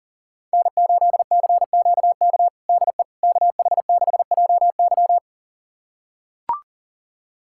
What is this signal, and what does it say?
Morse code, heard as a keyed tone, about 40 wpm, 700 Hz. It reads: N8CQK DE KH6JY
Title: USA Callsign DE Callsign - 5 Characters - 40wpm